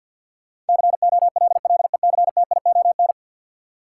Morse code, heard as a keyed tone, about 52 wpm, 700 Hz. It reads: COLLECTION
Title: Sentences from Top 1200 Words - New Word Drill - 50wpm